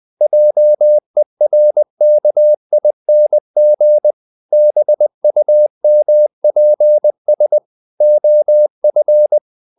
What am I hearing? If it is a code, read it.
Morse code, heard as a keyed tone, about 20 wpm, 600 Hz. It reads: JERKING BUMPS OF
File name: the old man and the sea part 2 20wpm